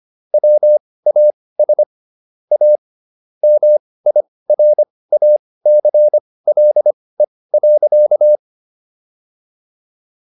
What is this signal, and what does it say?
Morse code, heard as a keyed tone, about 25 wpm, spaced as for 18 wpm, 600 Hz. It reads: WAS A MIRACLE.